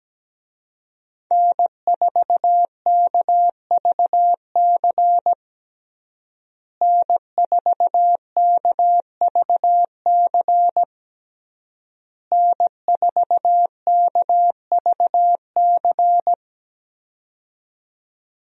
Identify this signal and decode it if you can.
Morse code, heard as a keyed tone, about 17 wpm, 700 Hz. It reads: N4KVC N4KVC N4KVC
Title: USA Call Signs - 5 Characters - Repeated 3x - 17wpm